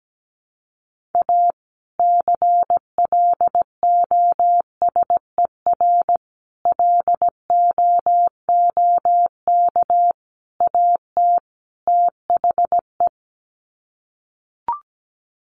Morse code, heard as a keyed tone, about 17 wpm, 700 Hz. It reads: A CLOSER LOOK AT THE
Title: Sets of 5 Words Encore 17wpm